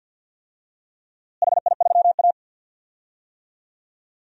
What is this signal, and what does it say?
Morse code, heard as a keyed tone, about 53 wpm, 700 Hz. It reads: HI3A
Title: DXCC-100 Call Signs - 4 Characters - Repeated 2x - 50wpm